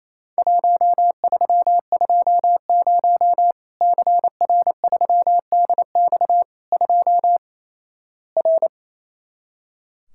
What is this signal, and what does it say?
Morse code, heard as a keyed tone, about 28 wpm, 700 Hz. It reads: 1320 CR3DX 2 R
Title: Worked All Europe DX Contest - QTC Exchange - 28wpm